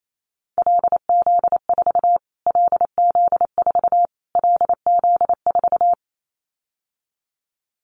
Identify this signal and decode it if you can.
Morse code, heard as a keyed tone, about 28 wpm, 700 Hz. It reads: LZ4 LZ4 LZ4
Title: Callsign Prefixes for Top 25 DXCC - Repeated 3x - 28wpm